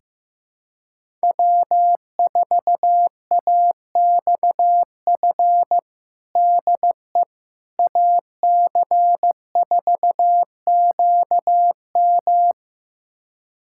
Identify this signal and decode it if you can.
Morse code, heard as a keyed tone, about 15 wpm, 700 Hz. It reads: W4AXF DE AC4QM
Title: USA Callsign DE Callsign - 5 Characters - 15wpm